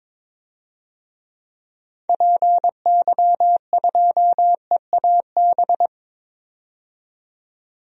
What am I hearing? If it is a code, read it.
Morse code, heard as a keyed tone, about 22 wpm, 700 Hz. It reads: PY2EAB
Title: DXCC-100 Call Signs - 6 Characters - 22wpm